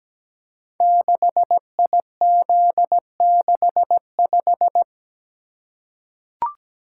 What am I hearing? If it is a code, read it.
Morse code, heard as a keyed tone, about 17 wpm, 700 Hz. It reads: 6IZ65
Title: Code Groups - 5 Letter:Numbers - 17wpm